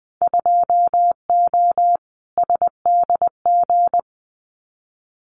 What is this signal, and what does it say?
Morse code, heard as a keyed tone, about 20 wpm, 700 Hz. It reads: 2O SDG
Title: Winter Field Day Exchanges 20wpm